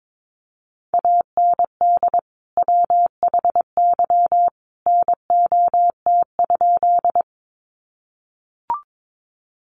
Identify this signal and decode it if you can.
Morse code, heard as a keyed tone, about 22 wpm, 700 Hz. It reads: AND WHY NOT?
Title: Sentences from Top 600 Words 22wpm